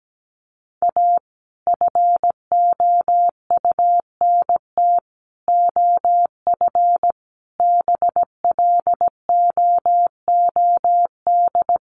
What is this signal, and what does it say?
Morse code, heard as a keyed tone, about 17 wpm, 700 Hz. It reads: A FOUNT OF BLOOD
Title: A Sound Of Thunder 17wpm